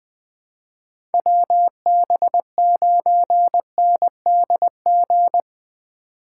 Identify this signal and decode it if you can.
Morse code, heard as a keyed tone, about 20 wpm, 700 Hz. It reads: WB9NDG
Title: 2023 POTA Hunter Call Signs - 20wpm